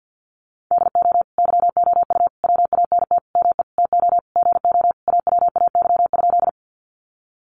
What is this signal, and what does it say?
Morse code, heard as a keyed tone, about 48 wpm, 700 Hz. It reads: DO YOU WANT ME TO GO AWAY?
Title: Sentences from Top 600 Words 50wpm